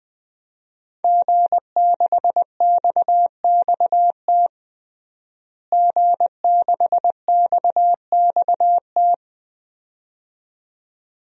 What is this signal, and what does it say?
Morse code, heard as a keyed tone, about 20 wpm, 700 Hz. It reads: G6XXT G6XXT
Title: DXCC-100 Call Signs - 5 Characters - Repeated 2x - 20wpm